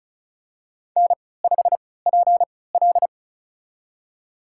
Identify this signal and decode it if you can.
Morse code, heard as a keyed tone, about 34 wpm, spaced as for 20 wpm, 700 Hz. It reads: N5PL
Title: USA Call Signs - 4 Characters - 3x Character Spacing - 35wpm